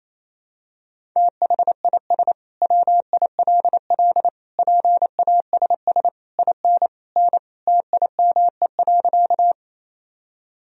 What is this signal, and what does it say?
Morse code, heard as a keyed tone, about 28 wpm, 700 Hz. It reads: THIS WILL PASS IN N TIME.
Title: Sentences from Top 400 Words and Abbreviations 28wpm